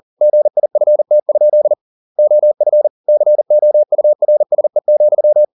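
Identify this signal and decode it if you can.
Morse code, heard as a keyed tone, about 40 wpm, 600 Hz. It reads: GIFT? OF COURSE,
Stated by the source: Morse code at 40 wpm, practice text, CW med Gnister.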